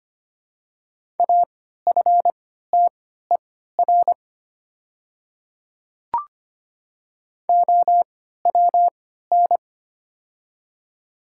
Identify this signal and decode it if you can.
Morse code, heard as a keyed tone, about 25 wpm, spaced as for 14 wpm, 700 Hz. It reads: AFTER OWN
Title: 3x Character Spacing - Top 300 Words - 25wpm